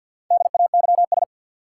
Morse code, heard as a keyed tone, about 49 wpm, 700 Hz. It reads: DAYS
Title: Sets of 5 Words Encore - New Word Drill - 50wpm